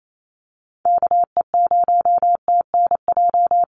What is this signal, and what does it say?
Morse code, heard as a keyed tone, about 28 wpm, 700 Hz. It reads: KE0TNJ
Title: USA Call Signs 28wpm